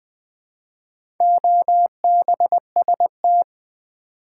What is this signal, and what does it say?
Morse code, heard as a keyed tone, about 20 wpm, 700 Hz. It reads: OBST
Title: Build A Word - 2 Letters At A Time - 8-Letter Words - 20wpm